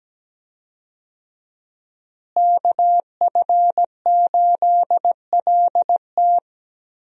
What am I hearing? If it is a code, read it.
Morse code, heard as a keyed tone, about 17 wpm, 700 Hz. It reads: KF8LT
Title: USA Call Signs - 5 Characters - Repeated 3x - 17wpm